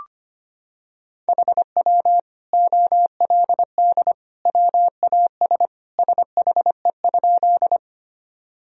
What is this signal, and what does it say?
Morse code, heard as a keyed tone, about 25 wpm, 700 Hz. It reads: HW OLD WAS SHE?
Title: Sentences from Top 200 Words and Abbreviations 25wpm